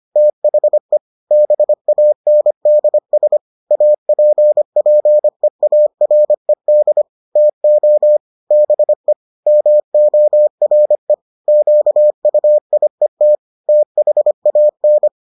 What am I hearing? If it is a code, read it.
Morse code, heard as a keyed tone, about 25 wpm, 600 Hz. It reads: THE BANDS APPEARED TO BE MORE QUIET THAN